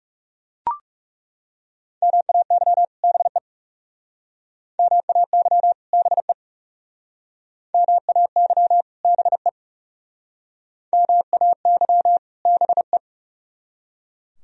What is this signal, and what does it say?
Morse code, heard as a keyed tone, about 40 wpm, 700 Hz. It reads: MAY BE MAY BE MAY BE MAY BE